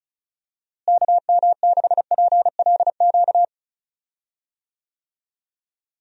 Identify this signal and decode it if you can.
Morse code, heard as a keyed tone, about 34 wpm, 700 Hz. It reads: KM6PLQ
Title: USA Call Signs - 6 Characters - Repeated 3x - 35wpm